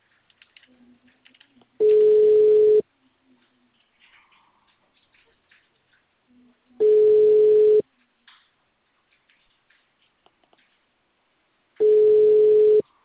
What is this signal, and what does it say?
Ringback tone (Europe, 425 Hz), with speech.